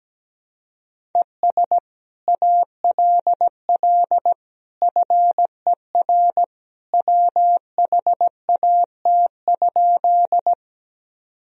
Morse code, heard as a keyed tone, about 17 wpm, 700 Hz. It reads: ES ALL FER WHAT?
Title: Sentences from Top 100 Words and Abbreviations 17wpm